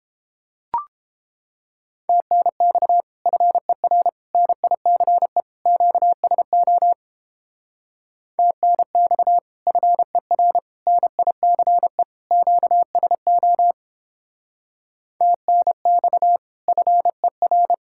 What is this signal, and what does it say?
Morse code, heard as a keyed tone, about 29 wpm, 700 Hz. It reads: TNX FER NICE QSO TNX FER NICE QSO TNX FER